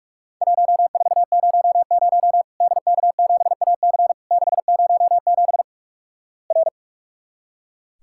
Morse code, heard as a keyed tone, about 46 wpm, 700 Hz. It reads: 1400 DK7AC 607 R
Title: Worked All Europe DX Contest - QTC Exchange - 45wpm